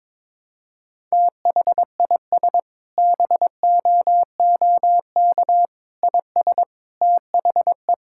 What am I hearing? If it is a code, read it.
Morse code, heard as a keyed tone, about 22 wpm, 700 Hz. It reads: THIS BOOK IS THE
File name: Sentences with 7-Word Phrases - 22wpm